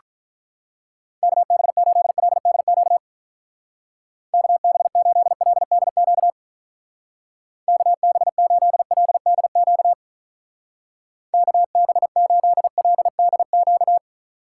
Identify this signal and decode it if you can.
Morse code, heard as a keyed tone, about 45 wpm, 700 Hz. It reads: KB8LDQ KB8LDQ KB8LDQ KB8LDQ